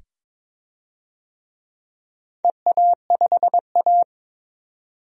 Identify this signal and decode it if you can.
Morse code, heard as a keyed tone, about 22 wpm, 700 Hz. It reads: EA5A
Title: DXCC-100 Call Signs - 4 Characters - 22wpm